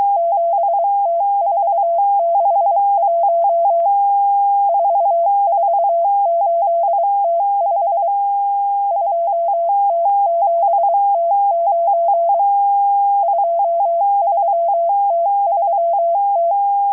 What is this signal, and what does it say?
Morse code, heard as a keyed tone, about 23 wpm, 680 Hz. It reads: 7T461 448T5 2T7T9 23T3T
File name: HM02- -0437utc-7351khz